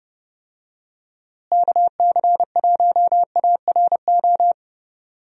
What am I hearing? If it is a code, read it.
Morse code, heard as a keyed tone, about 30 wpm, 700 Hz. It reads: KC1ARO
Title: USA Call Signs - 6 Characters - Repeated 3x - 30wpm